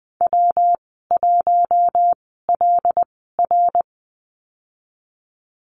Morse code, heard as a keyed tone, about 20 wpm, spaced as for 14 wpm, 700 Hz. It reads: W1LR
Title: 167 Lesson 27g - Number 7 - Call Signs 20wpm